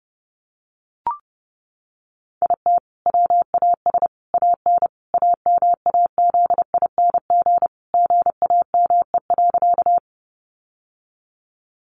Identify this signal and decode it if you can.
Morse code, heard as a keyed tone, about 30 wpm, 700 Hz. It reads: IT WAS AN AMAZING GAME.
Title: Sentences from Top 1500 Words 30wpm